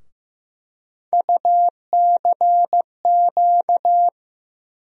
Morse code, heard as a keyed tone, about 15 wpm, 700 Hz. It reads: UCQ
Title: USA Call Sign Postfixes - 3 Letters 15wpm